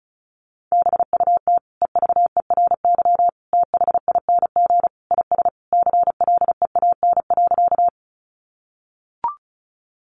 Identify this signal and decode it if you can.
Morse code, heard as a keyed tone, about 34 wpm, 700 Hz. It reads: BUT EVERYTHING IS CLEAN.
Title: Sentences from Top 1100 Words 35wpm